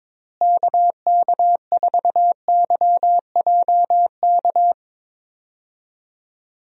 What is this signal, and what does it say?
Morse code, heard as a keyed tone, about 22 wpm, 700 Hz. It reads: KK4YJK
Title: USA Call Signs - 6 Characters - Repeated 2x - 22wpm